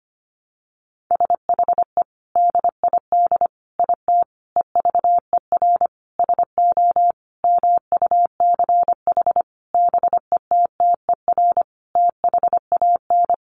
Morse code, heard as a keyed tone, about 25 wpm, 700 Hz. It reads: SHE DID IT EVER SO MUCH BETTER THAN
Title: Sentences from Top 400 Words 25wpm